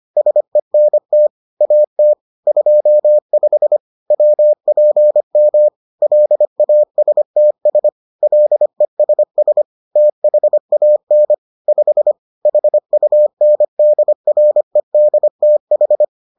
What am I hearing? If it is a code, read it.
Morse code, heard as a keyed tone, about 25 wpm, 600 Hz. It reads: SENT AT 25 WPM LASTS LESS THAN 5 HUNDREDTH